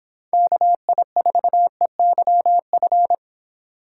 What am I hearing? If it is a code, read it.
Morse code, heard as a keyed tone, about 26 wpm, 700 Hz. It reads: KI4EYF